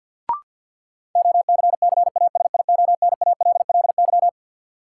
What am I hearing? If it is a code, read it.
Morse code, heard as a keyed tone, about 53 wpm, 700 Hz. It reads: OCCASIONALLY
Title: Sentences from Top 1300 Words - New Word Drill Alt - 50wpm